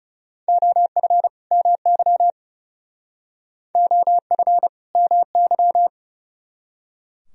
Morse code, heard as a keyed tone, about 37 wpm, 700 Hz. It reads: OF MY OF MY
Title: Speed-Racing - Sets of 2 Words 30wpm